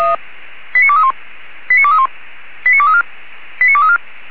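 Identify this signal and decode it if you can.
Five-tone selective call (ZVEI), heard as quick runs of five tones, tones of about 70 ms. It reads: 89231 89231 89234 89234
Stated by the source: ZVEI five-tone selective call received on an SDR.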